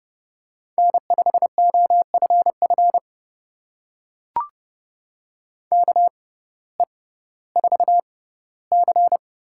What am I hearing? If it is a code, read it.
Morse code, heard as a keyed tone, about 30 wpm, spaced as for 17 wpm, 700 Hz. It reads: N5OFF KE4C
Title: USA Call Signs - 5 Characters - 6x Character Spacing - 30wpm